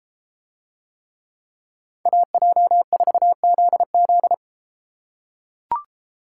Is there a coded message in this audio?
Yes: Morse code from 2.1 to 4.3 s, heard as a keyed tone, about 32 wpm, 700 Hz. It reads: AJ4ZZ